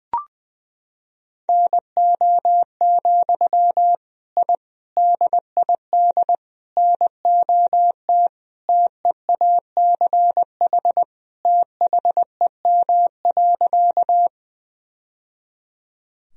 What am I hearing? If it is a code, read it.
Morse code, heard as a keyed tone, about 20 wpm, 700 Hz. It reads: NO, I DID NOT TEACH THEM.
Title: Sentences from Top 1000 Words - Review Part 2 - 20wpm